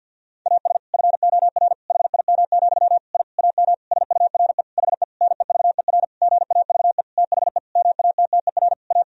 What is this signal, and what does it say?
Morse code, heard as a keyed tone, about 46 wpm, 700 Hz. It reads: AS FOR HIM, I AM SURE HE NEVER GAVE THE MATTER A
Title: Sentences from Top 1000 Words - Review Part 2 - 50wpm